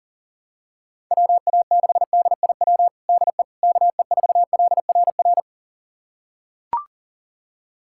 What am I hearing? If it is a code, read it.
Morse code, heard as a keyed tone, about 40 wpm, 700 Hz. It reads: WA6DIW DE KE4LRR
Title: USA Callsign DE Callsign 40wpm